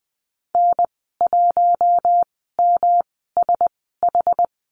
Morse code, heard as a keyed tone, about 20 wpm, spaced as for 14 wpm, 700 Hz. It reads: N1MSH
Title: 231 Lesson 37g - Letter x - Call Signs 20wpm